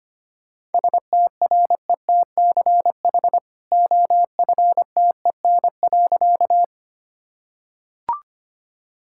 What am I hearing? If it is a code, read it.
Morse code, heard as a keyed tone, about 25 wpm, 700 Hz. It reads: STRETCH OFTEN.
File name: Easy Sentences with Hints - 25wpm